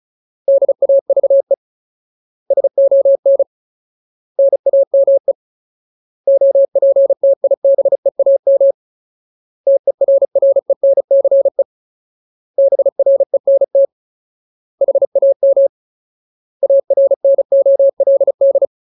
Morse code, heard as a keyed tone, about 34 wpm, 550 Hz. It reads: DAVE SON NAME OPTIBEAM TERRENCE BRENT HAM ARNOLD
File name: Ultimate Word List / 35WPM Ultimate